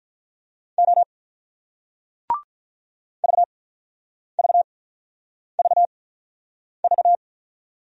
Morse code, heard as a keyed tone, about 45 wpm, spaced as for 9 wpm, 700 Hz. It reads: O VVVV